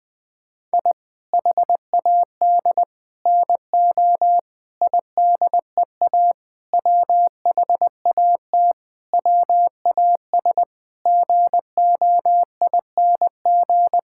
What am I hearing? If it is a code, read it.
Morse code, heard as a keyed tone, about 20 wpm, 700 Hz. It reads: I HAD NO IDEA WHAT WAS GOING
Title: Sentences with 8-Word Phrases - Spoken Beginning - 20wpm